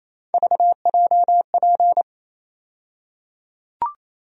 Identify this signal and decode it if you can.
Morse code, heard as a keyed tone, about 28 wpm, 700 Hz. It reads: VJP
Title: USA Call Sign Postfixes - 3 Letters 28wpm